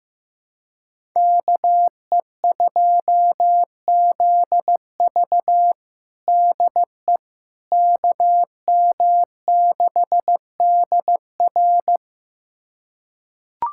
Morse code, heard as a keyed tone, about 15 wpm, 700 Hz. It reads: KE2ZV DE KM6DR
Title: USA Callsign DE Callsign - 5 Characters - 15wpm